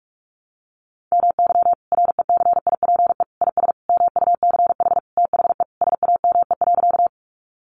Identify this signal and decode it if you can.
Morse code, heard as a keyed tone, about 44 wpm, 700 Hz. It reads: MY RECIPE IS MUCH THE SAME.
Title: Sentences from Top 1200 Words 45wpm